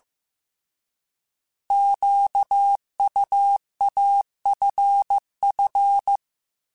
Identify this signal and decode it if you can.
Morse code, heard as a keyed tone, about 15 wpm, 800 Hz. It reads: QUAFF
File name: Alphabetical / Alphabetical mp3s / Qs @13